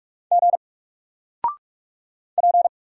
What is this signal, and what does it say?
Morse code, heard as a keyed tone, about 48 wpm, 700 Hz. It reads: G P
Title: Single Letter-Number Alt 45wpm